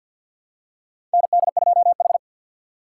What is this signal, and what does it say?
Morse code, heard as a keyed tone, about 53 wpm, 700 Hz. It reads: ND2H
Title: USA Call Signs - 4 Characters - 6x Character Spacing - 50wpm